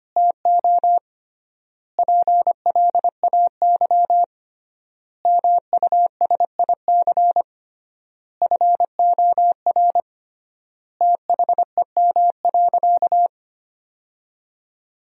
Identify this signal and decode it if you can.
Morse code, heard as a keyed tone, about 25 wpm, 700 Hz. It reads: TO PLAY MUSIC FOR THEM.
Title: 3x Word-Spacing - Sentences from Top 300 Words - 25wpm